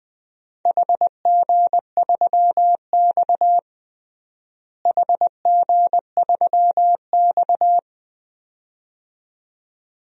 Morse code, heard as a keyed tone, about 20 wpm, 700 Hz. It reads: HG3X HG3X
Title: DXCC-100 Call Signs - 4 Characters - Repeated 2x - 20wpm